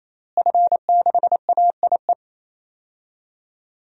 Morse code, heard as a keyed tone, about 28 wpm, 700 Hz. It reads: F6AIE
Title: DXCC-100 Call Signs - 5 Characters - Repeated 3x - 28wpm